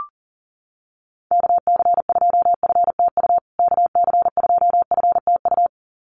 Morse code, heard as a keyed tone, about 40 wpm, 700 Hz. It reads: KC2FTU KC2FTU
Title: USA Call Signs Repeated 2x 40wpm